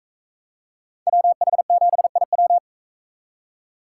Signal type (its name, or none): Morse code